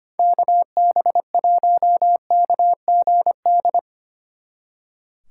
Morse code, heard as a keyed tone, about 33 wpm, 700 Hz. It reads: KB1KGD